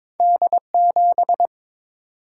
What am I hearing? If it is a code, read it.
Morse code, heard as a keyed tone, about 22 wpm, 700 Hz. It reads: D7
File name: practice / 22wpm / Two Letter-Number Combinations - 22wpm